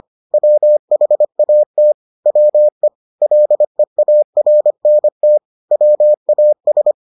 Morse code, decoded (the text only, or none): WHAT WE LEARNT WAS